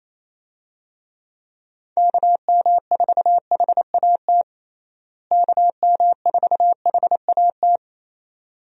Morse code, heard as a keyed tone, about 28 wpm, 700 Hz. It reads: KM4HAT KM4HAT
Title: USA Call Signs - 6 Characters - Repeated 2x - 28wpm